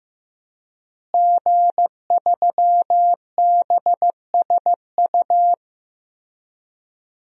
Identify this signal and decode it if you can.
Morse code, heard as a keyed tone, about 15 wpm, 700 Hz. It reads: G3BSU
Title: DXCC-100 Call Signs - 5 Characters - Repeated 3x - 15wpm